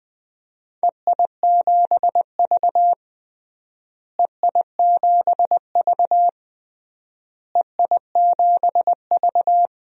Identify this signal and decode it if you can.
Morse code, heard as a keyed tone, about 20 wpm, 700 Hz. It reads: EI7V EI7V EI7V
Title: DXCC-100 Call Signs - 4 Characters - Repeated 3x - 20wpm